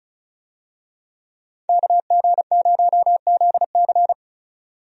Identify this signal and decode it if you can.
Morse code, heard as a keyed tone, about 36 wpm, 700 Hz. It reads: KG0ZC
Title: USA Call Signs - 5 Characters - 35wpm